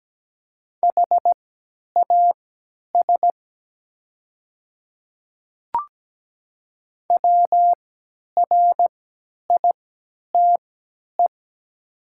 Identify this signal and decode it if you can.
Morse code, heard as a keyed tone, about 17 wpm, spaced as for 10 wpm, 700 Hz. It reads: HAS WRITE
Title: 3x Character Spacing - Top 500 Words - 17wpm